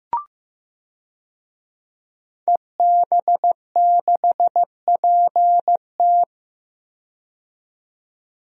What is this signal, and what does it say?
Morse code, heard as a keyed tone, about 15 wpm, 700 Hz. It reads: EB6PT
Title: DXCC-100 Call Signs - 5 Characters - 15wpm